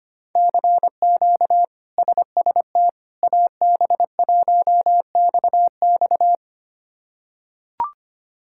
Morse code, heard as a keyed tone, about 25 wpm, 700 Hz. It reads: CQ SST AB1XX
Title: K1USN SST QSO Practice - 25wpm